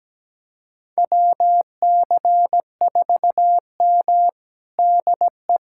Morse code, heard as a keyed tone, about 17 wpm, 700 Hz. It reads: WC4M DE
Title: USA Callsign DE Callsign 17wpm